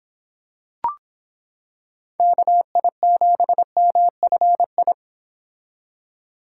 Morse code, heard as a keyed tone, about 26 wpm, 700 Hz. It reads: KI7MFI